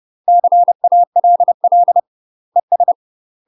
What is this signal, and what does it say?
Morse code, heard as a keyed tone, about 30 wpm, 700 Hz. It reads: CALL ES
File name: UK MASTER QSOs TRAINING / 30WPM / UK MASTER QSOs 0027